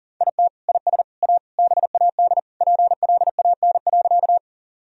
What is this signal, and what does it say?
Morse code, heard as a keyed tone, about 40 wpm, 700 Hz. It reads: IT IS A BAD PLAN.
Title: Sentences from Top 1000 Words - Review Part 2 - 40wpm